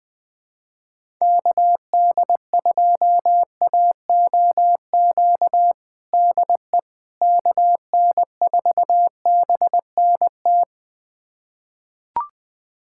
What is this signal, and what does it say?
Morse code, heard as a keyed tone, about 21 wpm, 700 Hz. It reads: KD2AOQ DE KN4BNT
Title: USA Callsign DE Callsign - 6 Characters - 20wpm